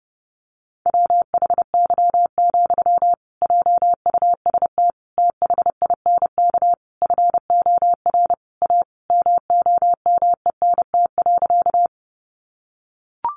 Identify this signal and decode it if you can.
Morse code, heard as a keyed tone, about 30 wpm, 700 Hz. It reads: WHY, JUST THINK FOR A MOMENT.
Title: Sentences from Top 1000 Words 30wpm